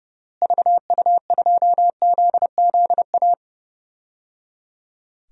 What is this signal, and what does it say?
Morse code, heard as a keyed tone, about 30 wpm, 700 Hz. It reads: VU2ZZA
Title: DXCC-100 Call Signs - 6 Characters - Repeated 3x - 30wpm